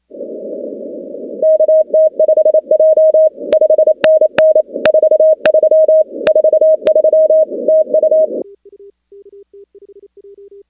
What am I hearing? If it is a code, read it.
Morse code, heard as a keyed tone, about 28 wpm, 630 Hz. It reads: KT5J 5NN 43 43 TU
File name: contests / 2019-cw-wpx-kt5j-k5pi- -2330 0-16